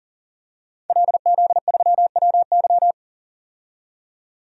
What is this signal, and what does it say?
Morse code, heard as a keyed tone, about 40 wpm, 700 Hz. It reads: LZ3WY
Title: DXCC-100 Call Signs - 5 Characters - 40wpm